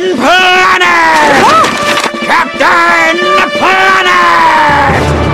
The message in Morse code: none